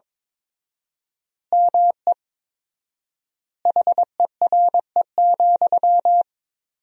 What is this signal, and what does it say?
Morse code, heard as a keyed tone, about 22 wpm, 700 Hz. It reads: ME HERE,
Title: 4x Word-Spacing - Sentences from Top 500 Words - 22wpm